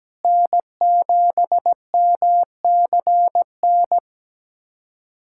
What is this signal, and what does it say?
Morse code, heard as a keyed tone, about 17 wpm, 700 Hz. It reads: N7MCN
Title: USA Callsign DE Callsign - 5 Characters - 17wpm